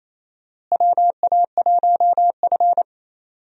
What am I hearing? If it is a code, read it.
Morse code, heard as a keyed tone, about 28 wpm, 700 Hz. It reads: WA1F
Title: USA Call Signs - 4 Characters - Repeated 2x - 28wpm